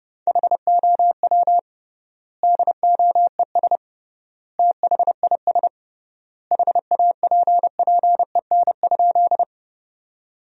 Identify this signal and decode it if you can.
Morse code, heard as a keyed tone, about 30 wpm, 700 Hz. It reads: HOW DOES THIS HAPPEN?
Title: 3x Word-Spacing - Sentences from Top 400 Words - 30wpm